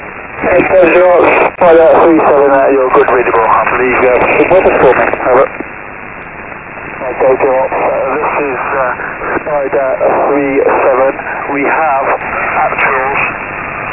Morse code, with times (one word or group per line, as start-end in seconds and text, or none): none